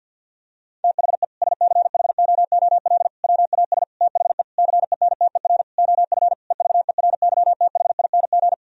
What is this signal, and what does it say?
Morse code, heard as a keyed tone, about 50 wpm, 700 Hz. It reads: THE SCHOOL WAS THE CENTER OF EVERYTHING
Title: Sentences from Top 1000 Words - Review Part 2 - 50wpm